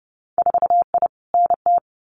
Morse code, heard as a keyed tone, about 30 wpm, 700 Hz. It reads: 4I NT
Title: Winter Field Day Exchanges Encore 30wpm